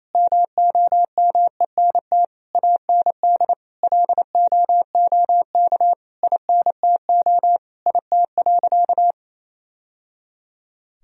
Morse code, heard as a keyed tone, about 28 wpm, 700 Hz. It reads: MOMENT AND LOOK INTO IT.
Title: Sentences from Top 1000 Words - Review Part 1 - 28wpm